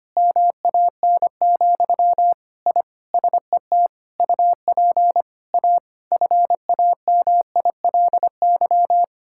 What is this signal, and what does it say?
Morse code, heard as a keyed tone, about 25 wpm, 700 Hz. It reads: MAN, I SET UP A FAMILY
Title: Sentences from Top 1000 Words - Review Part 5 - 25wpm